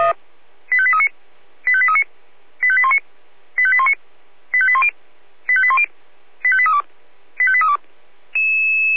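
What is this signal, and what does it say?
Five-tone selective call (ZVEI), heard as quick runs of five tones, tones of about 70 ms. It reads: 85839 85839 85819 85819 85810 85810 85832 85832
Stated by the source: Five-tone ZVEI selective call received on an SDR.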